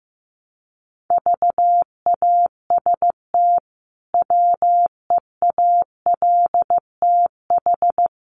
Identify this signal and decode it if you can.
Morse code, heard as a keyed tone, about 15 wpm, 700 Hz. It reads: VAST WEALTH
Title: Sentences from Top 1300 Words 15wpm